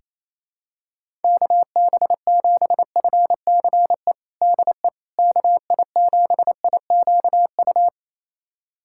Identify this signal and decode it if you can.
Morse code, heard as a keyed tone, about 28 wpm, 700 Hz. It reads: KB7FCE DE KI7IQU
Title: USA Callsign DE Callsign - 6 Characters - 28wpm